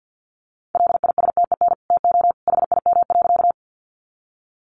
Morse code, heard as a keyed tone, about 50 wpm, 700 Hz. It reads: LISTEN TO HIM.
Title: Sentences from Top 400 Words 50wpm